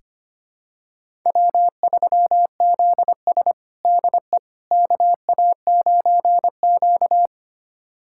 Morse code, heard as a keyed tone, about 25 wpm, 700 Hz. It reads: W3ZS DE KA9Q
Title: USA Callsign DE Callsign - 4 Characters - 25wpm